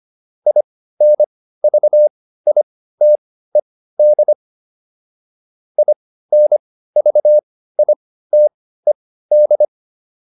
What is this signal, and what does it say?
Morse code, heard as a keyed tone, about 25 wpm, spaced as for 15 wpm, 600 Hz. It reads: INVITED INVITED